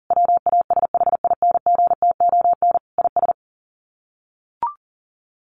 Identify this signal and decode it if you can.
Morse code, heard as a keyed tone, about 40 wpm, 700 Hz. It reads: WASHINGTON IS
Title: Sets of 5 Words Encore2 40wpm